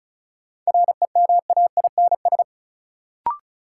Morse code, heard as a keyed tone, about 36 wpm, 700 Hz. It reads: REMAINS
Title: Sets of 5 Words Encore - New Word Drill - 35wpm